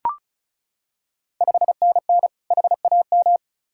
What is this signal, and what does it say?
Morse code, heard as a keyed tone, about 36 wpm, 700 Hz. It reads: 5NN HAM
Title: Ontario QSO Party Exchange 35wpm